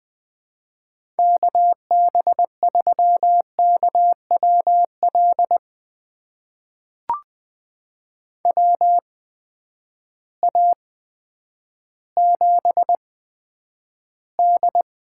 Morse code, heard as a keyed tone, about 21 wpm, spaced as for 10 wpm, 700 Hz. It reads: KB3KWL WA7D
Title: USA Call Signs - 6 Characters - 8x Character Spacing - 20wpm